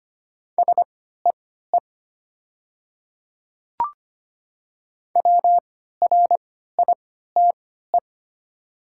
Morse code, heard as a keyed tone, about 25 wpm, spaced as for 14 wpm, 700 Hz. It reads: SEE WRITE
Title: 3x Character Spacing - Top 100 Words - 25wpm